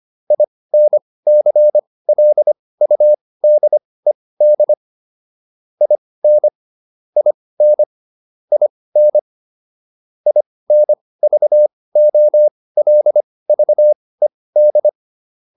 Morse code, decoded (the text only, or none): INCLUDED IN IN IN INVOLVED